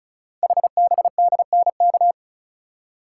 Morse code, heard as a keyed tone, about 36 wpm, 690 Hz. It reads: HBDNK
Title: Code Groups - 5 Letters - 35wpm